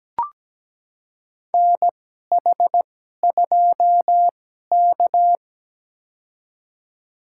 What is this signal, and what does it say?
Morse code, heard as a keyed tone, about 17 wpm, spaced as for 12 wpm, 700 Hz. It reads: NH2K
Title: USA Call Signs - 2x Character Spacing - 17wpm